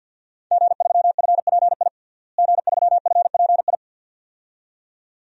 Morse code, heard as a keyed tone, about 52 wpm, 700 Hz. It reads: G3FPI G3FPI
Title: DXCC-100 Call Signs - 5 Characters - Repeated 2x - 50wpm